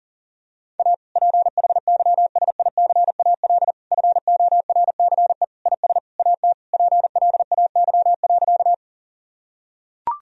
Morse code, heard as a keyed tone, about 40 wpm, 700 Hz. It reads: A PHYSICAL FORCE IS AT PLAY.
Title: Sentences from Top 1000 Words - Review Part 4 - 40wpm